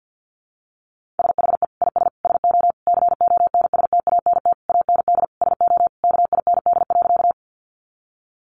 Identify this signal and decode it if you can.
Morse code, heard as a keyed tone, about 48 wpm, 700 Hz. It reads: SHE IS SO CONSTANT AND SO KIND.
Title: Sentences from Top 1500 Words 50wpm